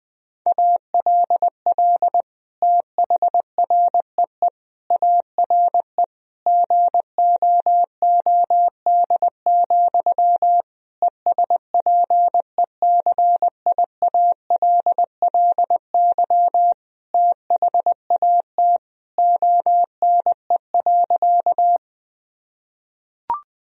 Morse code, heard as a keyed tone, about 20 wpm, 700 Hz. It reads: ALL THREE ARE GOOD, ESPECIALLY THAT ONE.
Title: Sentences from Top 1000 Words - Review Part 4 - 20wpm